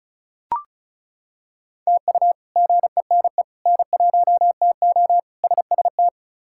Morse code, heard as a keyed tone, about 36 wpm, 700 Hz. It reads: TU GENE N1TO SST
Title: K1USN SST QSO Practice - 35wpm